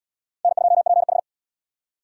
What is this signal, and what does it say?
Morse code, heard as a keyed tone, about 69 wpm, 700 Hz. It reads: N2PH